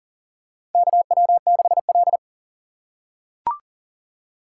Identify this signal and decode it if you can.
Morse code, heard as a keyed tone, about 37 wpm, 700 Hz. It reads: KW6L
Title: USA Call Signs - 2x Character Spacing - 40wpm